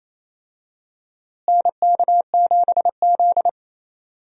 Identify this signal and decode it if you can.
Morse code, heard as a keyed tone, about 28 wpm, 700 Hz. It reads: NK7Z
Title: USA Call Signs - 4 Characters - Repeated 3x - 28wpm